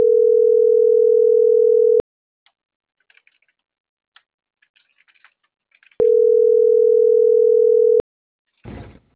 Ringback tone (North America), 440 + 480 Hz.